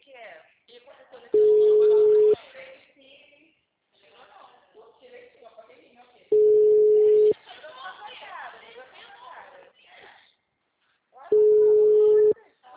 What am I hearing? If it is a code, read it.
Ringback tone (Europe, 425 Hz), with speech.
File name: out- -2014- - - .48